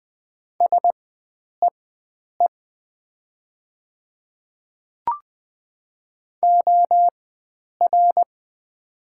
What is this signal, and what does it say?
Morse code, heard as a keyed tone, about 20 wpm, spaced as for 9 wpm, 700 Hz. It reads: SEE OR
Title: 4x Character Spacing - Top 100 Words - 20wpm